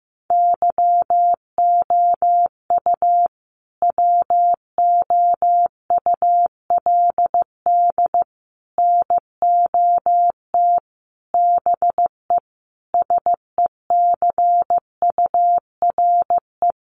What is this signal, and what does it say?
Morse code, heard as a keyed tone, about 15 wpm, 700 Hz. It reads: YOU WOULD NOT BE SECURE
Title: Sentences from Top 1200 Words 15wpm